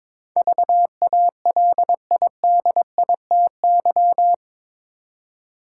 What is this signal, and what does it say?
Morse code, heard as a keyed tone, about 22 wpm, 700 Hz. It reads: VALIDITY
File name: practice / 22wpm / Eight Letter Words - 22wpm